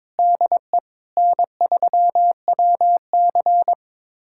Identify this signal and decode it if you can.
Morse code, heard as a keyed tone, about 22 wpm, 700 Hz. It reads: DE N3WC
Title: USA Callsign DE Callsign - 4 Characters - 22wpm